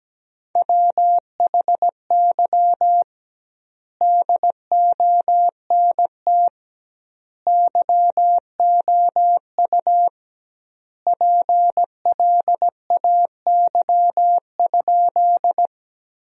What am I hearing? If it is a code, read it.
Morse code, heard as a keyed tone, about 17 wpm, 700 Hz. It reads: WHY DONT YOU PLAY?
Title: 2x Word-Spacing - Sentences from Top 300 Words - 17wpm